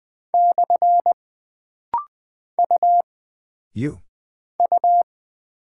Morse code, heard as a keyed tone, about 20 wpm, 700 Hz. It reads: / U U
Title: Single Character (Letter:Number:Punctuation:Prosign) - Rapid-Fire - 20wpm